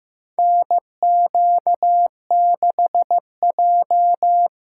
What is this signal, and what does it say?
Morse code, heard as a keyed tone, about 15 wpm, 700 Hz. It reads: NQ6J
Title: USA Call Signs - 4 Characters - 3x Character Spacing - 15wpm